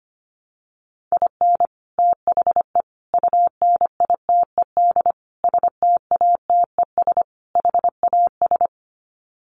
Morse code, heard as a keyed tone, about 25 wpm, 700 Hz. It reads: IN THE UNITED STATES HAS
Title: Sets of 5 Words Encore2 25wpm